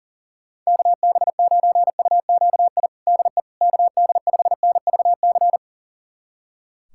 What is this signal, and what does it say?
Morse code, heard as a keyed tone, about 40 wpm, 700 Hz. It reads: KB9UQI DE KD5NVC
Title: USA Callsign DE Callsign - 6 Characters - 40wpm